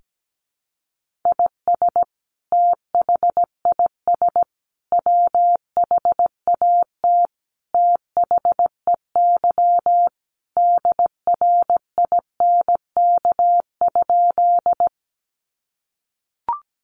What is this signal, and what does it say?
Morse code, heard as a keyed tone, about 17 wpm, 700 Hz. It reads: IS THIS WHAT THEY DRINK?
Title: Sentences from Top 1500 Words 17wpm